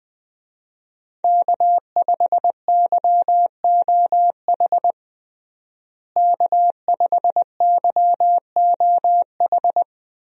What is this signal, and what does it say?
Morse code, heard as a keyed tone, about 20 wpm, 700 Hz. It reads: K5YOH K5YOH
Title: USA Call Signs - 5 Characters - Repeated 2x - 20wpm